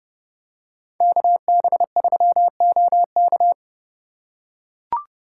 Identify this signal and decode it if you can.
Morse code, heard as a keyed tone, about 30 wpm, 700 Hz. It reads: KB3OK